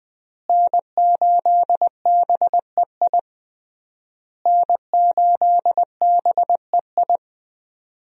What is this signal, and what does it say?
Morse code, heard as a keyed tone, about 20 wpm, 700 Hz. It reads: N8BEI N8BEI
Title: USA Call Signs - 5 Characters - Repeated 3x - 20wpm